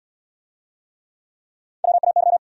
Morse code, heard as a keyed tone, about 63 wpm, 700 Hz. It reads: WAY